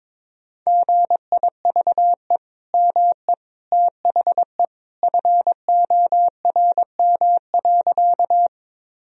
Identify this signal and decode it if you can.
Morse code, heard as a keyed tone, about 22 wpm, 700 Hz. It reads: GIVE ME THE FORM.
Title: Sentences from Top 1000 Words - Review Part 1 - 22wpm